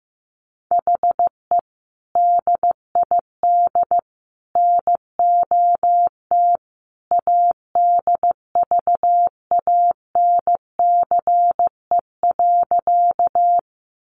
Morse code, heard as a keyed tone, about 15 wpm, 700 Hz. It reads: HE DID NOT ADVANCE.
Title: Sentences from Top 1400 Words 15wpm